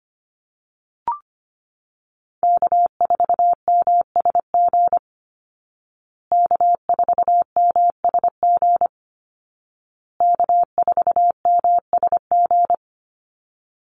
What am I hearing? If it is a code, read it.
Morse code, heard as a keyed tone, about 25 wpm, 700 Hz. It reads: K4MSG K4MSG K4MSG
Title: Call Signs Repeated 2x - 25wpm